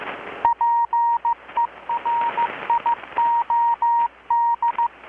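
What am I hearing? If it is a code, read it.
Morse code, heard as a keyed tone, about 15 wpm, 940 Hz. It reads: PERIOD
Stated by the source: Morse from KPH's Night of Nights XX maritime radiotelegraphy event, received in Chiapas, Mexico about 3,800 km away on a Kenwood R-600 with a grounded 10 m long-wire antenna.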